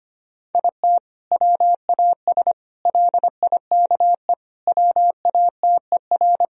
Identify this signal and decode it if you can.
Morse code, heard as a keyed tone, about 25 wpm, 700 Hz. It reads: IT WAS LIKE WATER
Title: Sentences from Top 1000 Words - Review Part 4 - 25wpm